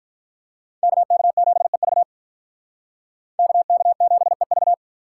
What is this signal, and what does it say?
Morse code, heard as a keyed tone, about 46 wpm, 700 Hz. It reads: KK7EV KK7EV